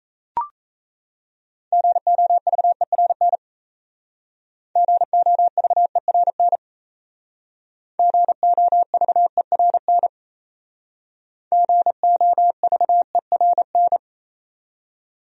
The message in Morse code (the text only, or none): GOVERN GOVERN GOVERN GOVERN